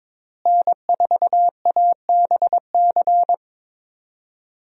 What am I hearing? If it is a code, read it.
Morse code, heard as a keyed tone, about 21 wpm, 700 Hz. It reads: N4ABC
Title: USA Call Signs - 5 Characters - Repeated 3x - 22wpm